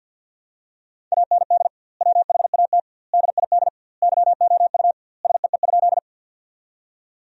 Morse code, heard as a keyed tone, about 50 wpm, 700 Hz. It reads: AND WHAT DID YOU SEE?
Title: 2x Word-Spacing - Sentences from Top 100 Words - 50wpm